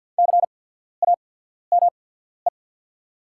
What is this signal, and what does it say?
Morse code, heard as a keyed tone, about 48 wpm, spaced as for 14 wpm, 700 Hz. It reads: CAME
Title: 8x Character Spacing - Top 300 Words - 50wpm